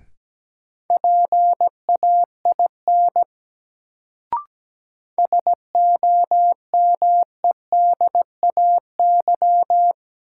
Morse code, heard as a keyed tone, about 17 wpm, 700 Hz. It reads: PAIN SOMEDAY
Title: Sentences from Top 1600 Words - New Word Drill Alt - 17wpm